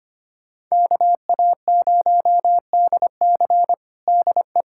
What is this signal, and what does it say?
Morse code, heard as a keyed tone, about 25 wpm, 700 Hz. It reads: KA0DC DE
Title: Practice QSO - Exchange 4 - Ending - 25wpm